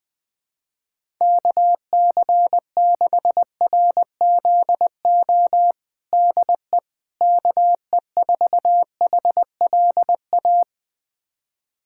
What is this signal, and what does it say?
Morse code, heard as a keyed tone, about 20 wpm, 700 Hz. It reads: KC6RZO DE KE4HLA
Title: USA Callsign DE Callsign - 6 Characters - 20wpm